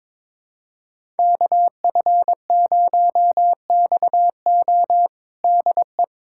Morse code, heard as a keyed tone, about 22 wpm, 700 Hz. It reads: KF0XO DE
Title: USA Callsign DE Callsign - 5 Characters - 22wpm